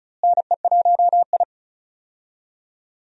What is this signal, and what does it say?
Morse code, heard as a keyed tone, about 34 wpm, 700 Hz. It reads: NE1I
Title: USA Call Signs - 4 Characters - Repeated 2x - 35wpm